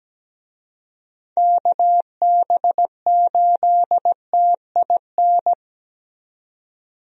Morse code, heard as a keyed tone, about 17 wpm, 700 Hz. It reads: KB8TIN
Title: USA Call Signs - 6 Characters - 17wpm